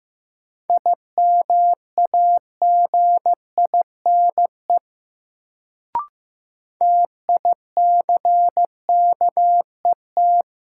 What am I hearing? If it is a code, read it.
Morse code, heard as a keyed tone, about 15 wpm, 700 Hz. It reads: IMAGINE TICKET
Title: Sentences from Top 1400 Words - New Word Drill - 15wpm